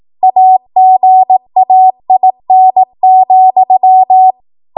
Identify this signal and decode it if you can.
Morse code, heard as a keyed tone, about 18 wpm, 750 Hz. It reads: AGAIN,
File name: Morse / 18WPM